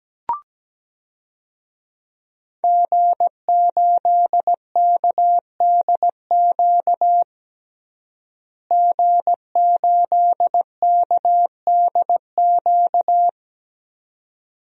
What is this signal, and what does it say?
Morse code, heard as a keyed tone, about 17 wpm, 700 Hz. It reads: G8KDQ G8KDQ
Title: DXCC-100 Call Signs - 5 Characters - Repeated 2x - 17wpm